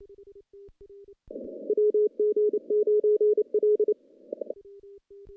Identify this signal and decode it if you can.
Morse code, heard as a keyed tone, about 28 wpm, 420 Hz. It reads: WG9L
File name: contests / CW-SS-k5tr- -1300